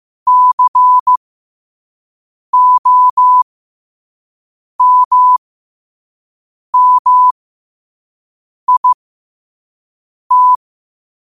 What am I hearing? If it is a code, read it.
Morse code, heard as a keyed tone, about 15 wpm, spaced as for 5 wpm, 1000 Hz. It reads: COMMIT
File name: qotdfiles / 5wpm